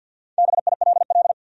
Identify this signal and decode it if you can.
Morse code, heard as a keyed tone, about 53 wpm, 700 Hz. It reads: BILL
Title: USA Common Names - Max 5 Letters - 50wpm